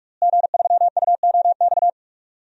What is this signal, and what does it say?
Morse code, heard as a keyed tone, about 41 wpm, 700 Hz. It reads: G3UOX
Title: DXCC-100 Call Signs - 5 Characters - Repeated 2x - 45wpm